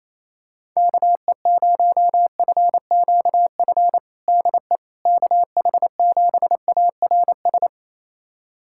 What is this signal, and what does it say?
Morse code, heard as a keyed tone, about 28 wpm, 700 Hz. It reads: KE0FQF DE KH7ARS
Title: USA Callsign DE Callsign - 6 Characters - 28wpm